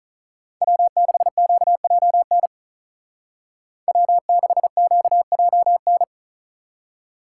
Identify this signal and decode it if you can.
Morse code, heard as a keyed tone, about 46 wpm, 700 Hz. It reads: W6QJN W6QJN